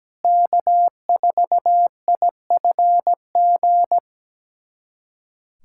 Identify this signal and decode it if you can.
Morse code, heard as a keyed tone, about 17 wpm, 700 Hz. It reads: K4IFG
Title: USA Callsign DE Callsign - 5 Characters - 17wpm